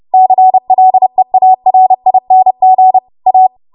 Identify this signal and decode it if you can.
Morse code, heard as a keyed tone, about 30 wpm, 750 Hz. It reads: CLEARING A